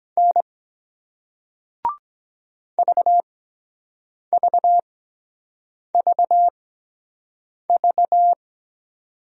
Morse code, heard as a keyed tone, about 23 wpm, spaced as for 7 wpm, 700 Hz. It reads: N VVVV